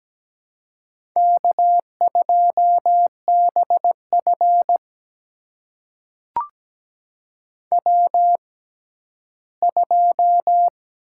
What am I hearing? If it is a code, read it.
Morse code, heard as a keyed tone, about 17 wpm, spaced as for 12 wpm, 700 Hz. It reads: K2BF W2
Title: USA Call Signs - 4 Characters - 6x Character Spacing - 17wpm